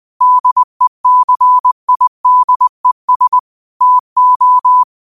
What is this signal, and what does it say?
Morse code, heard as a keyed tone, about 20 wpm, 1000 Hz. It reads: DECIDES TO